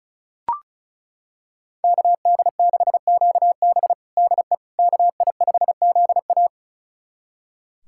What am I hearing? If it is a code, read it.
Morse code, heard as a keyed tone, about 36 wpm, 700 Hz. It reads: KD6QB DE KI5ZA
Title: USA Callsign DE Callsign - 5 Characters - 35wpm